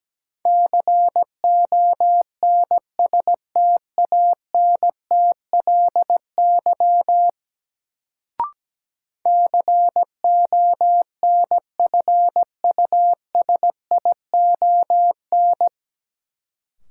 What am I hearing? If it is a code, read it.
Morse code, heard as a keyed tone, about 17 wpm, 700 Hz. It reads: CONSTANTLY CONFUSION
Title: Sentences from Top 1600 Words - New Word Drill Alt - 17wpm